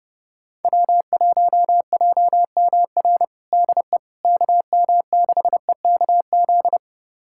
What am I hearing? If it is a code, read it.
Morse code, heard as a keyed tone, about 32 wpm, 700 Hz. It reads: W1JMR DE KM6EKZ
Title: USA Callsign DE Callsign 30wpm